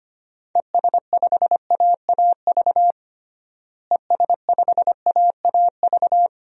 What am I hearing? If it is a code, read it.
Morse code, heard as a keyed tone, about 25 wpm, 700 Hz. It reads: ES5AAV ES5AAV
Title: DXCC-100 Call Signs - 6 Characters - Repeated 3x - 25wpm